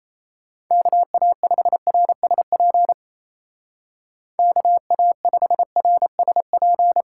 Morse code, heard as a keyed tone, about 35 wpm, 700 Hz. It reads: KA5RSP KA5RSP
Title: Speed Racing - USA Call Signs 28wpm